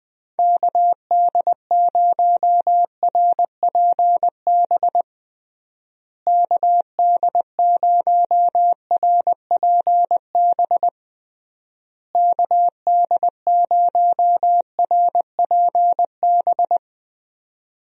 Morse code, heard as a keyed tone, about 21 wpm, 700 Hz. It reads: KD0RPB KD0RPB KD0RPB
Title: USA Call Signs - 6 Characters - Repeated 3x - 20wpm